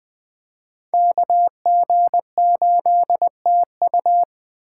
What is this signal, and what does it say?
Morse code, heard as a keyed tone, about 20 wpm, 700 Hz. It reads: KG8TU
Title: Speed Racing - USA 2x2 Call Signs 17wpm